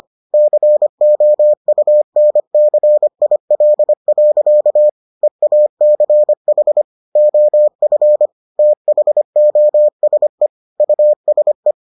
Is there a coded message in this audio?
Yes: Morse code from 0.3 to 11.7 s, heard as a keyed tone, about 25 wpm, 600 Hz. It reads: COUNCIL. EACH OF THOSE USE